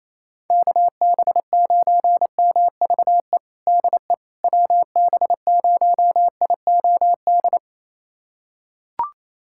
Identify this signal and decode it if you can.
Morse code, heard as a keyed tone, about 28 wpm, 700 Hz. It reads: KB9MVE DE WB0IOD
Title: USA Callsign DE Callsign 28wpm